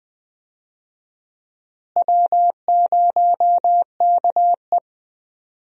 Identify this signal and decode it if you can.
Morse code, heard as a keyed tone, about 21 wpm, 700 Hz. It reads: W0KE
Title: USA Call Signs - 4 Characters - Repeated 2x - 20wpm